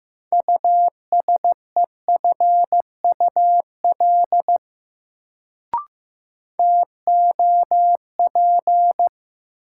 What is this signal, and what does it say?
Morse code, heard as a keyed tone, about 16 wpm, 700 Hz. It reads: USEFUL TOP
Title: Sentences from Top 700 Words - New Word Drill - 15wpm